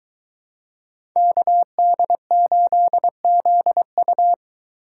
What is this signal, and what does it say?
Morse code, heard as a keyed tone, about 22 wpm, 700 Hz. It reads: KD8ZU